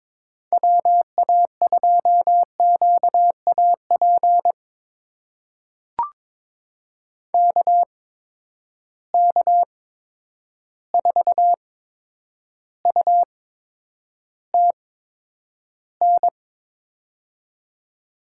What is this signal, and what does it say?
Morse code, heard as a keyed tone, about 22 wpm, spaced as for 9 wpm, 700 Hz. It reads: WA2QAP KK4UTN
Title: USA Call Signs - 6 Characters - 8x Character Spacing - 22wpm